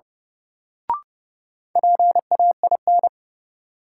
Morse code, heard as a keyed tone, about 30 wpm, 700 Hz. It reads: PAIN
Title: Sentences from Top 1600 Words - New Word Drill Alt - 30wpm